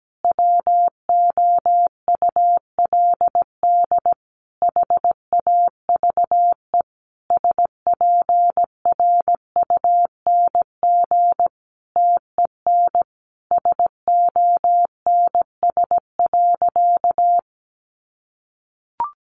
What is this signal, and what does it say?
Morse code, heard as a keyed tone, about 17 wpm, 700 Hz. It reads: WOULD HAVE SPRUNG TEN SONS.
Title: A Sound Of Thunder 17wpm